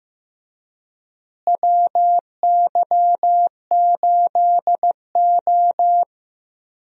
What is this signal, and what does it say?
Morse code, heard as a keyed tone, about 15 wpm, 700 Hz. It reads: WY8O
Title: USA Call Signs - 4 Characters - Repeated 3x - 15wpm